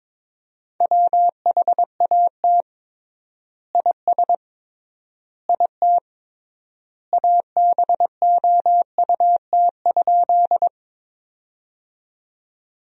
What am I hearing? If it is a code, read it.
Morse code, heard as a keyed tone, about 22 wpm, 700 Hz. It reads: WHAT IS IT ABOUT?
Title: 3x Word-Spacing - Sentences from Top 100 Words - 22wpm